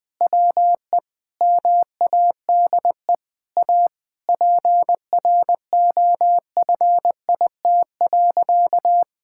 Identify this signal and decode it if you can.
Morse code, heard as a keyed tone, about 20 wpm, 700 Hz. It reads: WE MADE A PROFIT.
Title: Sentences from Top 1000 Words - Review Part 5 - 20wpm